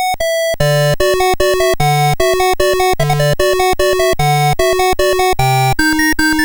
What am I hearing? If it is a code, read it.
There is no code.